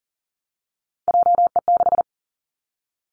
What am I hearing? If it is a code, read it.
Morse code, heard as a keyed tone, about 42 wpm, 700 Hz. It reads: JE6
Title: Three Challenging Letter-Number Combinations 40wpm